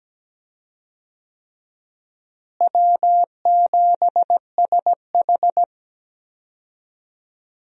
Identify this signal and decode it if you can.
Morse code, heard as a keyed tone, about 17 wpm, 700 Hz. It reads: W7SH
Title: USA Call Signs - 4 Characters - Repeated 3x - 17wpm